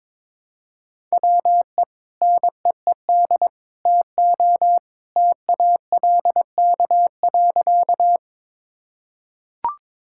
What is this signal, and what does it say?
Morse code, heard as a keyed tone, about 21 wpm, 700 Hz. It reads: WE NEED TO TALK.
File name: practice / 22wpm / Sentences with 3-Word Phrases - 22wpm